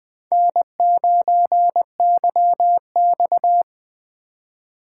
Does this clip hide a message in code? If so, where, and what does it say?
Yes: Morse code from 0.3 to 3.6 s, heard as a keyed tone, about 21 wpm, 700 Hz. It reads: N9YX